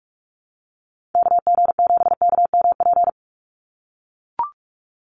Morse code, heard as a keyed tone, about 44 wpm, 700 Hz. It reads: KG7KMP
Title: USA Call Signs 45wpm